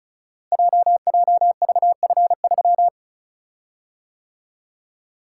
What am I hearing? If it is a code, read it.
Morse code, heard as a keyed tone, about 36 wpm, 690 Hz. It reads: JJVF3
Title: Code Groups - 5 Letter:Numbers - 35wpm